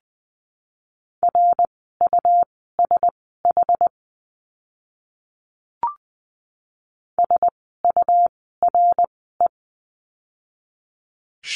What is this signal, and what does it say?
Morse code, heard as a keyed tone, about 20 wpm, spaced as for 14 wpm, 700 Hz. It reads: RUSH SURE
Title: 101 Lesson 17f - Letter u - Words 20wpm